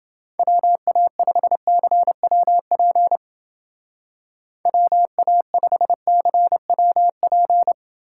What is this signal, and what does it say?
Morse code, heard as a keyed tone, about 29 wpm, 700 Hz. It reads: WA5CWP WA5CWP